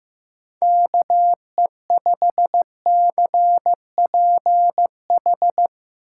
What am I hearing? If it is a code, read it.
Morse code, heard as a keyed tone, about 15 wpm, 700 Hz. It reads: KE5CPH
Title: Speed Racing - USA 2x3 Call Signs 15wpm